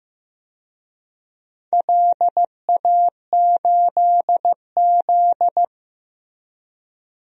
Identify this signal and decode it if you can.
Morse code, heard as a keyed tone, about 15 wpm, 700 Hz. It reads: LA8Z
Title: DXCC-100 Call Signs - 4 Characters - Repeated 2x - 15wpm